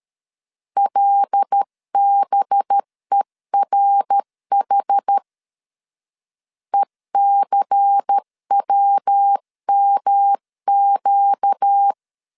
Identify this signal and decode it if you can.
Morse code, heard as a keyed tone, about 13 wpm, 800 Hz. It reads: LBERH ECWMQ
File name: CW / DET / 1 6 1500C S C 12wpm 3533